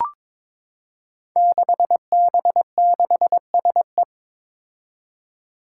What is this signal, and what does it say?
Morse code, heard as a keyed tone, about 22 wpm, 700 Hz. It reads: 6B6SE
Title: Five Challenging Letter-Number Combinations 22wpm